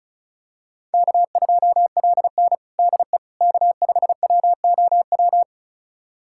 Morse code, heard as a keyed tone, about 34 wpm, 700 Hz. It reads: K2LN DE K5WOW
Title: USA Callsign DE Callsign 35wpm